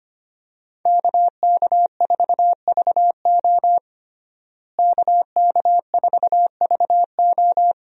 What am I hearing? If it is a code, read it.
Morse code, heard as a keyed tone, about 25 wpm, 700 Hz. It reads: KK4VO KK4VO
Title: USA Call Signs - 5 Characters - Repeated 2x - 25wpm